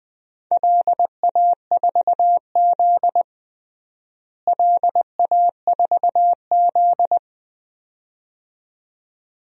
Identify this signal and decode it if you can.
Morse code, heard as a keyed tone, about 20 wpm, 700 Hz. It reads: LA4Z LA4Z
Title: DXCC-100 Call Signs - 4 Characters - Repeated 2x - 20wpm